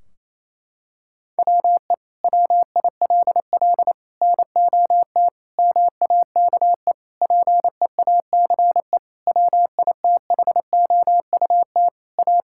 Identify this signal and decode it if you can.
Morse code, heard as a keyed tone, about 28 wpm, 700 Hz. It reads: WE WILL NOT MAKE PEACE WITHOUT A
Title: Sentences from Top 1600 Words 28wpm